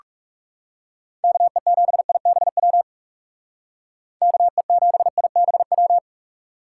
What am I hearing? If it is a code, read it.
Morse code, heard as a keyed tone, about 39 wpm, 700 Hz. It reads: KE7IBW KE7IBW